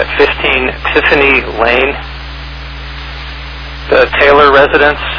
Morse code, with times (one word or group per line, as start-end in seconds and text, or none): none